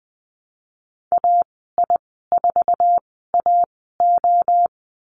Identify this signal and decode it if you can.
Morse code, heard as a keyed tone, about 20 wpm, spaced as for 15 wpm, 700 Hz. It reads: AI4AO
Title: 069 Lesson 12g - Letter d - Call Signs 20wpm